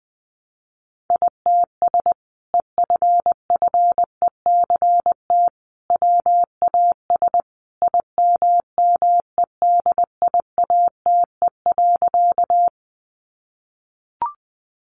Morse code, heard as a keyed tone, about 20 wpm, 700 Hz. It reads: ITS EFFECT WAS IMMEDIATE.
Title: Sentences from Top 1200 Words 20wpm